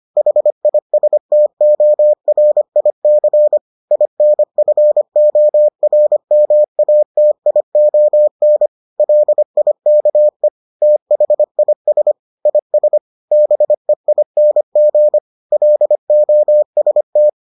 Morse code, heard as a keyed tone, about 25 wpm, 600 Hz. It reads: HISTORIC INFORMATION LIKE THIS IS BEING LOST